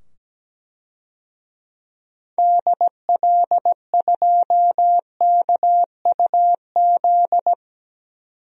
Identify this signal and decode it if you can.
Morse code, heard as a keyed tone, about 17 wpm, 700 Hz. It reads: DL2KUZ
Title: DXCC-100 Call Signs - 6 Characters - 17wpm